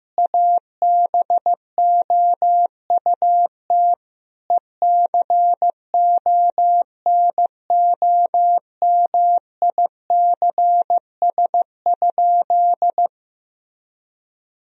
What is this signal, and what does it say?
Morse code, heard as a keyed tone, about 15 wpm, 700 Hz. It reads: ABOUT ECONOMICS?
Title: Sentences from Top 1000 Words - Review Part 3 - 15wpm